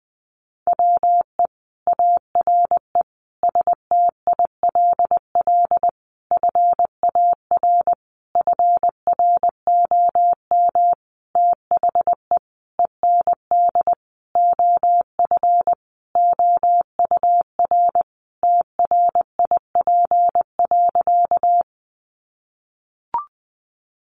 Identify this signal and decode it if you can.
Morse code, heard as a keyed tone, about 20 wpm, 700 Hz. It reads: WE ARE STILL FAR FROM THE END OF OUR TRIP.
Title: Sentences from Top 1300 Words 20wpm